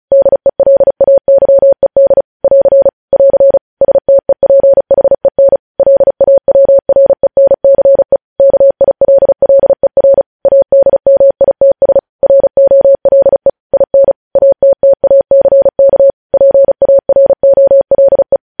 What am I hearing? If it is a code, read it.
Morse code, heard as a keyed tone, about 36 wpm, 550 Hz. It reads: DELAYED <AR> <AR> STEPHEN LAWRENCE KILLER ADMITS ROLE IN ATTACK PAROLE
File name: podcast / News headlines-1x-35x35